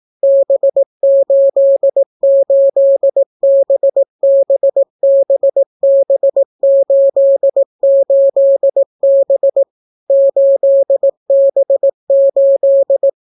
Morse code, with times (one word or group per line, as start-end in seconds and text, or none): 0.2-9.6 B88BBBB88B
10.1-13.1 8B8